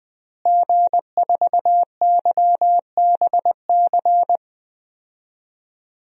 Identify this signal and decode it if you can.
Morse code, heard as a keyed tone, about 20 wpm, 700 Hz. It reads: G4YBC
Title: DXCC-100 Call Signs - 5 Characters - Repeated 2x - 20wpm